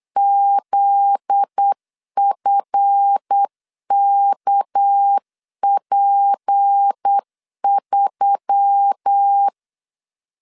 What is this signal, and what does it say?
Morse code, heard as a keyed tone, about 8 wpm, 800 Hz. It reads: ZFKP3
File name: LEZIONI / 9 28 8wpm 3512